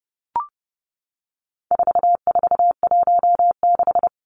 Morse code, heard as a keyed tone, about 30 wpm, 700 Hz. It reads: 4416
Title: Numbers - 4-Digit 30wpm